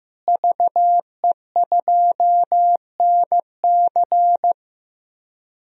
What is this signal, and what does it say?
Morse code, heard as a keyed tone, about 15 wpm, 700 Hz. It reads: VE2NC
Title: DXCC-100 Call Signs - 5 Characters - Repeated 2x - 15wpm